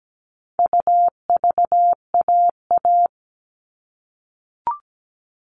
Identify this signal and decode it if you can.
Morse code, heard as a keyed tone, about 17 wpm, 700 Hz. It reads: UVAA
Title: All Cut Numbers - 4-Digit 17wpm